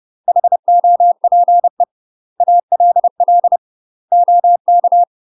Morse code, heard as a keyed tone, about 30 wpm, 700 Hz. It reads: HOPE ALL OK
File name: UK MASTER QSOs TRAINING / 30WPM / UK MASTER QSOs 0021